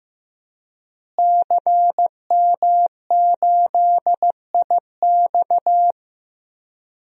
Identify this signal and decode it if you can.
Morse code, heard as a keyed tone, about 16 wpm, 700 Hz. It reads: CM8IX
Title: DXCC-100 Call Signs - 5 Characters - Repeated 2x - 15wpm